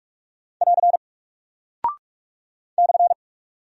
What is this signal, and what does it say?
Morse code, heard as a keyed tone, about 45 wpm, 700 Hz. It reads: <AR> /
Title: Single Character (Letter:Number:Punctuation:Prosign) Alt 45wpm